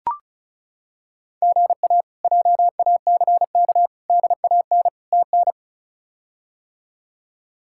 Morse code, heard as a keyed tone, about 34 wpm, 700 Hz. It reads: GA JACK DAN TN
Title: K1USN SST QSO Practice - 35wpm